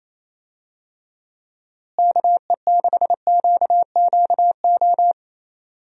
Morse code, heard as a keyed tone, about 27 wpm, 700 Hz. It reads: KE6QQO
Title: USA Call Signs - 6 Characters - Repeated 3x - 28wpm